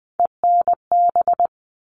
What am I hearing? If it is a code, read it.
Morse code, heard as a keyed tone, about 20 wpm, 700 Hz. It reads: ENB
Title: All Cut Numbers - 3-Digit 20wpm